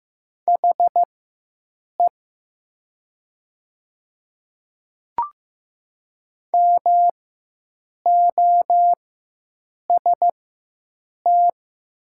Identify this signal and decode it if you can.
Morse code, heard as a keyed tone, about 15 wpm, spaced as for 7 wpm, 700 Hz. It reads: HE MOST
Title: 4x Character Spacing - Top 100 Words - 15wpm